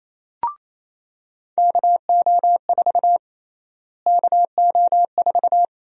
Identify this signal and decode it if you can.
Morse code, heard as a keyed tone, about 28 wpm, 700 Hz. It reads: KO4 KO4
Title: USA Call Sign Prefixes - Repeated 2x - 28wpm